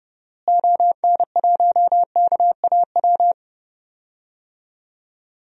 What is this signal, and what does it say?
Morse code, heard as a keyed tone, about 30 wpm, 700 Hz. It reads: ON1KAW
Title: DXCC-100 Call Signs - 6 Characters - Repeated 3x - 30wpm